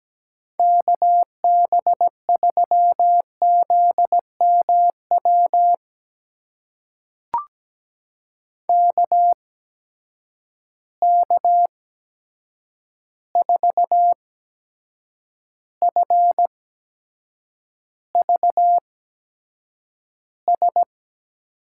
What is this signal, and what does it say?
Morse code, heard as a keyed tone, about 17 wpm, spaced as for 7 wpm, 700 Hz. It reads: KB3ZMW KK4FVS
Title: USA Call Signs - 6 Characters - 8x Character Spacing - 17wpm